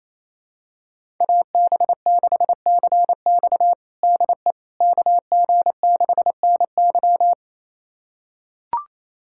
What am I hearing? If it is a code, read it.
Morse code, heard as a keyed tone, about 28 wpm, 700 Hz. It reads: AB6CX DE KG6NY
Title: USA Callsign DE Callsign - 5 Characters - 28wpm